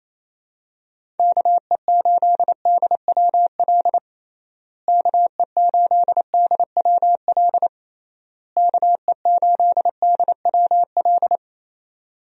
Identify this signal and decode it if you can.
Morse code, heard as a keyed tone, about 28 wpm, 700 Hz. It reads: KE8DWL KE8DWL KE8DWL
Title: USA Call Signs - 6 Characters - Repeated 3x - 28wpm